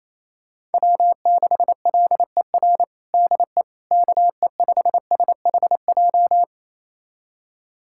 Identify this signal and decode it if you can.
Morse code, heard as a keyed tone, about 28 wpm, 700 Hz. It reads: W6LER DE KE5SHJ
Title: USA Callsign DE Callsign 28wpm